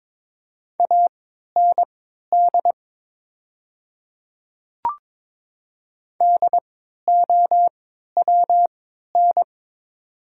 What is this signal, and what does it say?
Morse code, heard as a keyed tone, about 22 wpm, spaced as for 12 wpm, 700 Hz. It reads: AND DOWN
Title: 3x Character Spacing - Top 200 Words - 22wpm